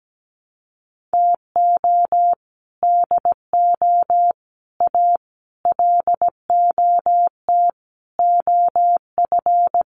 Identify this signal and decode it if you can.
Morse code, heard as a keyed tone, about 17 wpm, 700 Hz. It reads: TO DO A LOT OF
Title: Sets of 5 Words Encore 17wpm